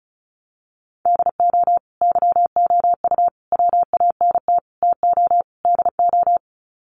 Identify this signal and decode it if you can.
Morse code, heard as a keyed tone, about 34 wpm, 700 Hz. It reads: DO YOU WANT TO DO
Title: Sets of 5 Words Encore 35wpm